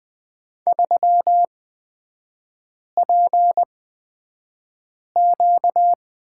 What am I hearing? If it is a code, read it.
Morse code, heard as a keyed tone, about 20 wpm, spaced as for 5 wpm, 700 Hz. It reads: 3PQ
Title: 255 Lesson 41e - Period - Three Characters 20wpm